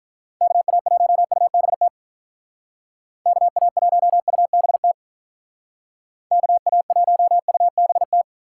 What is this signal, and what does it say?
Morse code, heard as a keyed tone, about 43 wpm, 700 Hz. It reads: KA1UBT KA1UBT KA1UBT
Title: Speed Racing - USA 2x3 Call Signs 35wpm